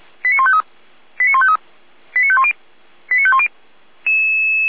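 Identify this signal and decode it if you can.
Five-tone selective call (ZVEI), heard as quick runs of five tones, tones of about 70 ms. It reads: 89153 89153 89510 89510
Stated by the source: ZVEI five-tone call 89165 received by SDR.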